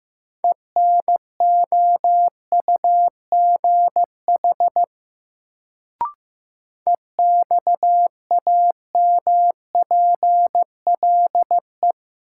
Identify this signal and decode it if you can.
Morse code, heard as a keyed tone, about 15 wpm, 700 Hz. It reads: ENOUGH EXAMPLE
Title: Sentences from Top 400 Words - New Word Drill - 15wpm